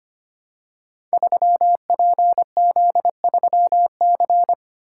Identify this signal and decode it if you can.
Morse code, heard as a keyed tone, about 25 wpm, 700 Hz. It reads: 3PZ3C
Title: Code Groups - 5 Letter:Numbers - 25wpm